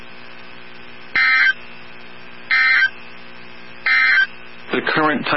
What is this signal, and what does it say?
Emergency Alert System (SAME) tones, heard as harsh buzzing bursts. It is heard from 1.0 to 4.3 s.
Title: NOAA Weather Radio 162.550